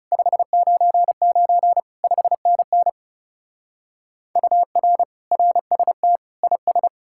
Morse code, heard as a keyed tone, about 35 wpm, 700 Hz. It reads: 599 5NN UR RST IS
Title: Speed-Racing Common QSO Phrases - 30wpm